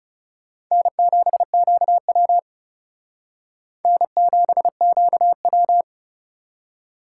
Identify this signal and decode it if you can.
Morse code, heard as a keyed tone, about 37 wpm, 700 Hz. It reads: N7QW N7QW
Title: Speed Racing - USA Call Signs 30wpm